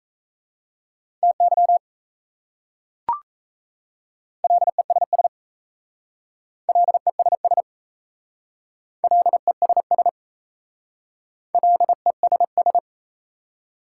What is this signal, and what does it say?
Morse code, heard as a keyed tone, about 37 wpm, 690 Hz. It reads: TY LESS LESS LESS LESS